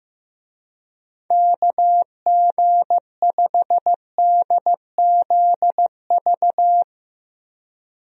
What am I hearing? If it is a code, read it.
Morse code, heard as a keyed tone, about 15 wpm, 700 Hz. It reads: KG5DZV
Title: USA Call Signs - 6 Characters - 15wpm